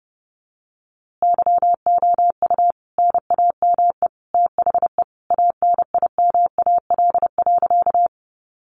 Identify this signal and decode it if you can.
Morse code, heard as a keyed tone, about 30 wpm, 700 Hz. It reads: YOU NAME THE ANIMAL.
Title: A Sound Of Thunder 30wpm